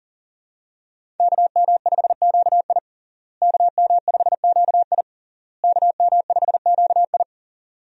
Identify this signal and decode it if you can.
Morse code, heard as a keyed tone, about 40 wpm, 700 Hz. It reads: KM5QI KM5QI KM5QI
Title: USA Call Signs - 5 Characters - Repeated 3x - 40wpm